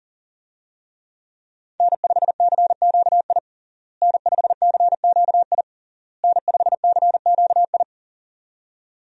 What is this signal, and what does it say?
Morse code, heard as a keyed tone, about 40 wpm, 700 Hz. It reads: N5CQI N5CQI N5CQI
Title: USA Call Signs - 5 Characters - Repeated 3x - 40wpm